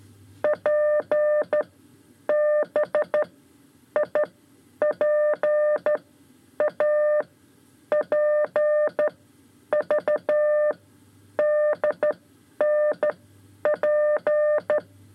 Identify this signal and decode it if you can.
Morse code, heard as a keyed tone, about 12 wpm, spaced as for 9 wpm, 600 Hz. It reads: PBIPA PVDNP
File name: 5 Lezione CW 2